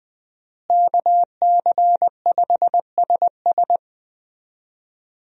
Morse code, heard as a keyed tone, about 20 wpm, 700 Hz. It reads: KC5SS
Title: USA Call Signs - 5 Characters - Repeated 2x - 20wpm